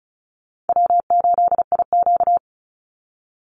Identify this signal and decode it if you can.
Morse code, heard as a keyed tone, about 34 wpm, 700 Hz. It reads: W8IQ
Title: Call Signs Repeated 3x - 35wpm